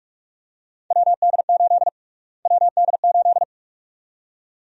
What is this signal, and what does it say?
Morse code, heard as a keyed tone, about 43 wpm, 700 Hz. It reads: WD8 WD8
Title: USA Call Sign Prefixes - Repeated 2x - 45wpm